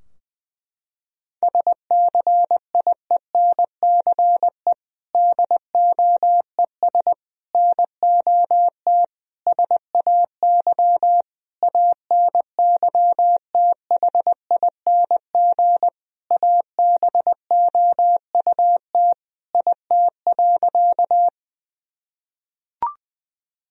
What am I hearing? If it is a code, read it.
Morse code, heard as a keyed tone, about 20 wpm, 700 Hz. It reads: SCIENCE DOES NOT SAY ANYTHING ABOUT IT.
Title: Sentences from Top 1000 Words - Review Part 2 - 20wpm